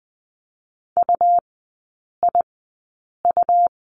Morse code, heard as a keyed tone, about 20 wpm, spaced as for 8 wpm, 700 Hz. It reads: UIU
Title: 100 Lesson 17e - Letter u - Three Characters 20wpm